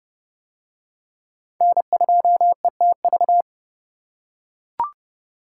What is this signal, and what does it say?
Morse code, heard as a keyed tone, about 25 wpm, 700 Hz. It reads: N2ETV
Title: Speed Racing - USA 1x3 Call Signs 20wpm